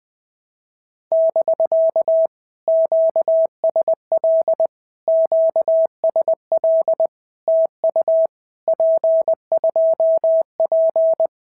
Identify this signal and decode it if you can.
Morse code, heard as a keyed tone, about 20 wpm, 650 Hz. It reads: <BK> QSL QSL TU P2P
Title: POTA QSO Practice - 20wpm